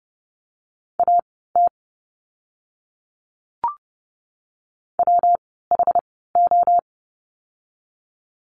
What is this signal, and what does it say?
Morse code, heard as a keyed tone, about 30 wpm, spaced as for 17 wpm, 700 Hz. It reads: AT WHO
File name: course / 30:10wpm / 230 Lesson 37f - Letter x - Words 30wpm